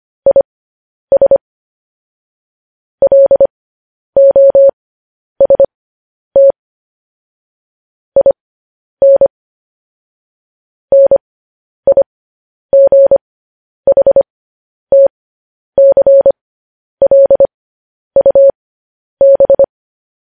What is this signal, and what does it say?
Morse code, heard as a keyed tone, about 25 wpm, spaced as for 10 wpm, 550 Hz. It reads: IS LOST IN NIGHTCLUB
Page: news headlines in morse code at 10 wpm